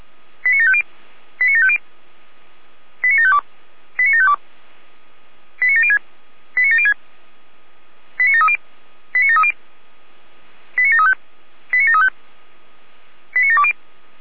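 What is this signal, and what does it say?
Five-tone selective call (ZVEI), heard as quick runs of five tones, tones of about 80 ms. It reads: 89750 89750 89752 89752 89706 89706 89730 89730 89735 89735 89720
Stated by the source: ZVEI five-tone call 89490 received on SDR.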